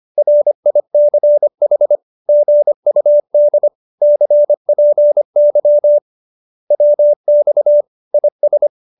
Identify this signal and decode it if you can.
Morse code, heard as a keyed tone, about 25 wpm, 570 Hz. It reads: RICH GUD CPY WX IS